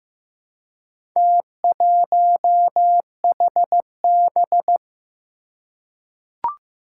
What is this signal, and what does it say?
Morse code, heard as a keyed tone, about 15 wpm, 700 Hz. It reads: T1HB
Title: Four Challenging Letter-Number Combinations 15wpm